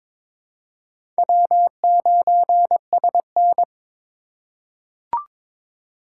Morse code, heard as a keyed tone, about 22 wpm, 700 Hz. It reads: W9SN
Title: USA Call Signs - 4 Characters - 2x Character Spacing - 22wpm